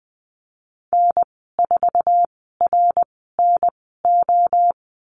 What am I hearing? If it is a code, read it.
Morse code, heard as a keyed tone, about 20 wpm, spaced as for 14 wpm, 700 Hz. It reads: N4RNO
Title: 055 Lesson 10g - Letter r - Call Signs 20wpm